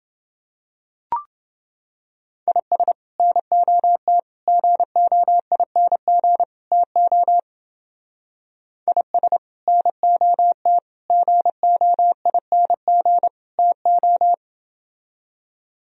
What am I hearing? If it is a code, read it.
Morse code, heard as a keyed tone, about 30 wpm, 700 Hz. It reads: IS NOT GOING TO IS NOT GOING TO